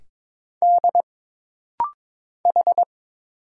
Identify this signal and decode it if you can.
Morse code, heard as a keyed tone, about 22 wpm, 700 Hz. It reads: D H
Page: This rapid-fire timing is geared to push your ICR (Instant Character Recognition) skills.